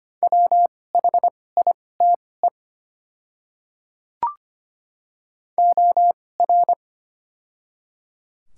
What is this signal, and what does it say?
Morse code, heard as a keyed tone, about 25 wpm, spaced as for 18 wpm, 700 Hz. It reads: WHITE OR
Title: 2x Character Spacing - Top 300 Words - 25wpm